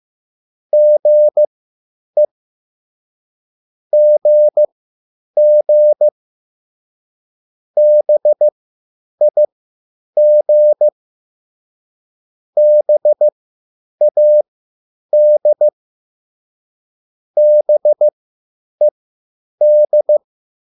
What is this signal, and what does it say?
Morse code, heard as a keyed tone, about 15 wpm, spaced as for 9 wpm, 600 Hz. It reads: GE GG BIG BAD BED
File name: QSL / CW lesson